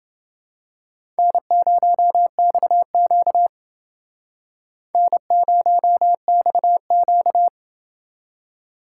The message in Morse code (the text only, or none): N0XQ N0XQ